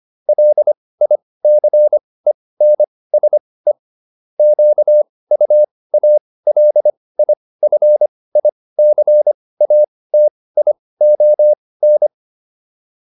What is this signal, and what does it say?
Morse code, heard as a keyed tone, about 25 wpm, spaced as for 18 wpm, 600 Hz. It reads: LICENSE QUALIFICATION